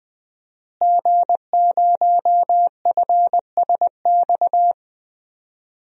Morse code, heard as a keyed tone, about 20 wpm, 700 Hz. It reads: G0FSX
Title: DXCC-100 Call Signs - 5 Characters - Repeated 3x - 20wpm